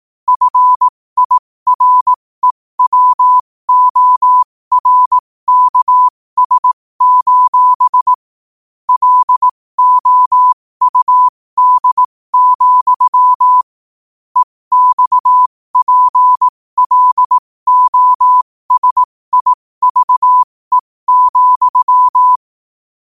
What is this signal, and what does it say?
Morse code, heard as a keyed tone, about 18 wpm, spaced as for 15 wpm, 1000 Hz. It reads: FIREWORKS: LOUD, EXPLOSIVE,
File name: qotdfiles / 15wpm